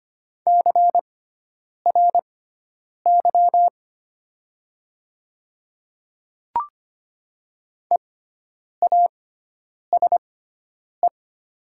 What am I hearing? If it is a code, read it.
Morse code, heard as a keyed tone, about 25 wpm, spaced as for 9 wpm, 700 Hz. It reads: CRY EASE
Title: 6x Character Spacing - Top 500 Words - 25wpm